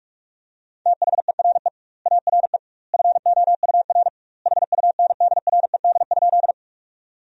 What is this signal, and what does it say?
Morse code, heard as a keyed tone, about 46 wpm, 700 Hz. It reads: THERE ARE FOUR HUNDRED?
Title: 2x Word-Spacing - Sentences from Top 400 Words - 45wpm